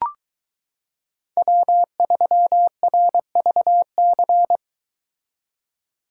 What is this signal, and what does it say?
Morse code, heard as a keyed tone, about 22 wpm, 700 Hz. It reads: W3RVC